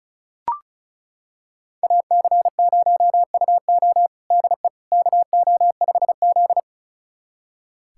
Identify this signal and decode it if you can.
Morse code, heard as a keyed tone, about 34 wpm, 700 Hz. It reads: AC0UO DE KO5Z
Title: USA Callsign DE Callsign 35wpm